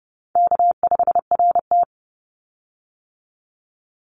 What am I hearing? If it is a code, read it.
Morse code, heard as a keyed tone, about 30 wpm, 700 Hz. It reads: K5RT
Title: Easy Call Signs - 30wpm